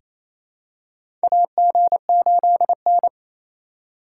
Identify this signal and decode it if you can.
Morse code, heard as a keyed tone, about 28 wpm, 700 Hz. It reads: AG8N
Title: USA Call Signs - 4 Characters - 4x Character Spacing - 28wpm